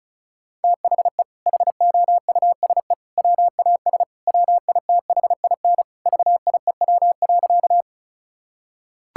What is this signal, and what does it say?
Morse code, heard as a keyed tone, about 36 wpm, 700 Hz. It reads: THE HOUSE WAS WITHIN VIEW.
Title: Sentences from Top 1000 Words - Review Part 6 - 35wpm